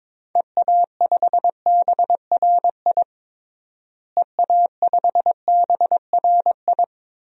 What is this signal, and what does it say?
Morse code, heard as a keyed tone, about 21 wpm, 700 Hz. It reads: EA5BRI EA5BRI
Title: DXCC-100 Call Signs - 6 Characters - Repeated 3x - 22wpm